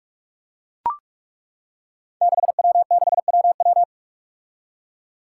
Morse code, heard as a keyed tone, about 42 wpm, 700 Hz. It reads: 6W6WW
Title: Five Challenging Letter-Number Combinations 45wpm